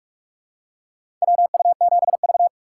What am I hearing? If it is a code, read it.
Morse code, heard as a keyed tone, about 46 wpm, 700 Hz. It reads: WU7V
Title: USA Call Signs - 4 Characters - 2x Character Spacing - 45wpm